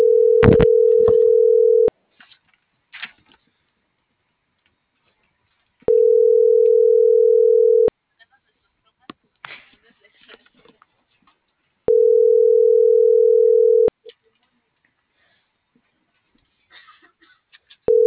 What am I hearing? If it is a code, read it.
Ringback tone (North America), 440 + 480 Hz.